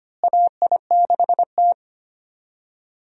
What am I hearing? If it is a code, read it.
Morse code, heard as a keyed tone, about 25 wpm, 700 Hz. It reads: AI6T
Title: USA Call Signs - 4 Characters - Repeated 3x - 25wpm